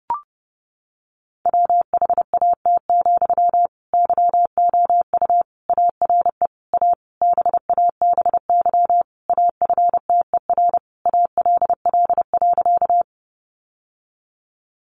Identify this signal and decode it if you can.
Morse code, heard as a keyed tone, about 30 wpm, 700 Hz. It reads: WHAT, YOU ARE A BABY AFTER ALL.
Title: Sentences from Top 1300 Words 30wpm